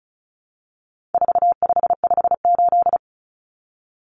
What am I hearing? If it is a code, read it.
Morse code, heard as a keyed tone, about 34 wpm, 700 Hz. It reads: 4558
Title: Numbers - 4-Digit 35wpm